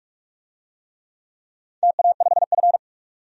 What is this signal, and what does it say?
Morse code, heard as a keyed tone, about 46 wpm, 700 Hz. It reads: TA5F
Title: DXCC-100 Call Signs - 4 Characters - 45wpm